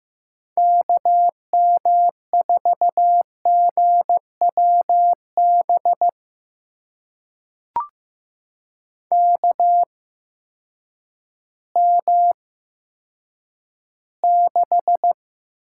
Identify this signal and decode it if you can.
Morse code, heard as a keyed tone, about 15 wpm, spaced as for 9 wpm, 700 Hz. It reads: KM4GWB KM6
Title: USA Call Signs - 6 Characters - 8x Character Spacing - 15wpm